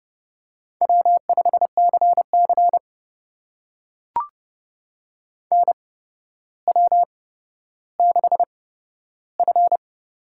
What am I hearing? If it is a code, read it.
Morse code, heard as a keyed tone, about 31 wpm, spaced as for 13 wpm, 700 Hz. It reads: W5CC NW6F
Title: USA Call Signs - 4 Characters - 8x Character Spacing - 30wpm